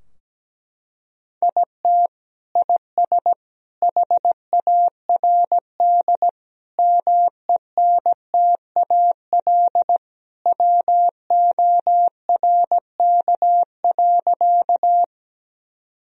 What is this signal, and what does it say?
Morse code, heard as a keyed tone, about 17 wpm, 700 Hz. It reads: IT IS HARD MENTAL WORK.
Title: Sentences from Top 1000 Words - Review Part 4 - 17wpm